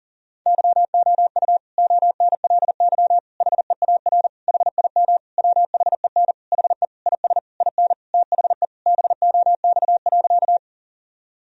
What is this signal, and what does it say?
Morse code, heard as a keyed tone, about 40 wpm, 700 Hz. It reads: YOU ONLY HEAR HIM WHEN HE IS IN THE BOX.
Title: Sentences from Top 1000 Words - Review Part 2 - 40wpm